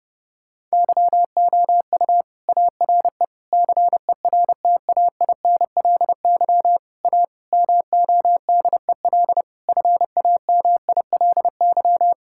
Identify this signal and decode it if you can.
Morse code, heard as a keyed tone, about 30 wpm, 700 Hz. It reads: YOU ARE CERTAINLY A MODEL FAMILY
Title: Sentences from Top 1000 Words - Review Part 5 - 30wpm